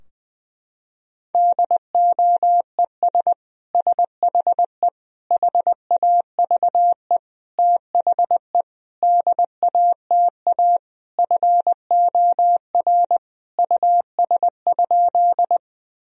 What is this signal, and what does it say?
Morse code, heard as a keyed tone, about 20 wpm, 700 Hz. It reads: DOES SHE HAVE THE DATA FOR US?
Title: Sentences from Top 1000 Words - Review Part 3 - 20wpm